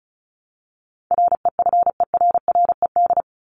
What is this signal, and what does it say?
Morse code, heard as a keyed tone, about 34 wpm, 700 Hz. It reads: REFERRED
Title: ED Suffix - Top 100 Words with Base Words 35wpm